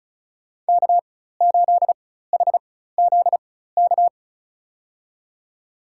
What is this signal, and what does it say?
Morse code, heard as a keyed tone, about 36 wpm, spaced as for 16 wpm, 700 Hz. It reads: K8HZK
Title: USA Call Signs - 5 Characters - 4x Character Spacing - 35wpm